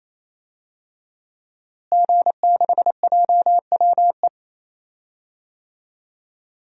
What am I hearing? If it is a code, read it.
Morse code, heard as a keyed tone, about 28 wpm, 700 Hz. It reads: G6JWE
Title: DXCC-100 Call Signs - 5 Characters - Repeated 3x - 28wpm